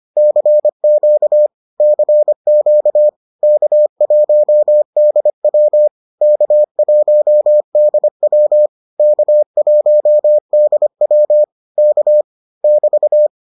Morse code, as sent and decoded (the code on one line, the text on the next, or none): -.-. --.- / -.-. --.- / -.- .---- -.. .-- / -.- .---- -.. .-- / -.- .---- -.. .-- / -.- / -...-
CQ CQ K1DW K1DW K1DW K <BT>